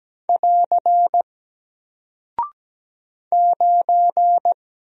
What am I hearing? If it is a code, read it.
Morse code, heard as a keyed tone, about 17 wpm, 700 Hz. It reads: <AR> 9
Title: Single Character (Letter:Number:Punctuation:Prosign) Alt 17wpm